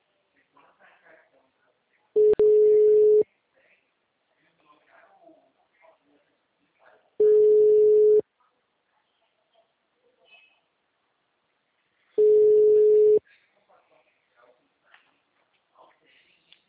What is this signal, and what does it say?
Ringback tone (Europe, 425 Hz), with speech.